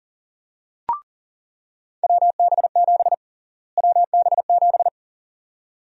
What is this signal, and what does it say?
Morse code, heard as a keyed tone, about 37 wpm, 700 Hz. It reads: WB7 WB7
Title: USA Call Sign Prefixes - Repeated 2x - 40wpm